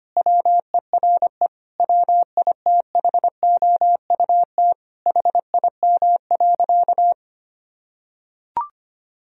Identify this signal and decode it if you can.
Morse code, heard as a keyed tone, about 25 wpm, 700 Hz. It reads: WERE WITHOUT HIM.
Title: Sentences from Top 1000 Words - Review Part 3 - 25wpm